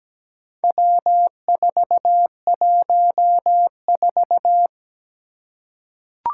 Morse code, heard as a keyed tone, about 17 wpm, 700 Hz. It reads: W414
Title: Four Challenging Letter-Number Combinations 17wpm